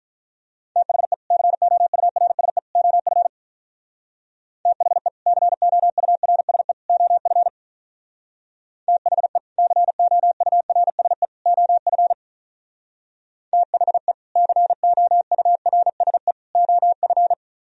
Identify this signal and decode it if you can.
Morse code, heard as a keyed tone, about 44 wpm, 700 Hz. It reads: THE COURSE OF THE COURSE OF THE COURSE OF THE COURSE OF
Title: Speed-Racing - Sets of 3 Words 35wpm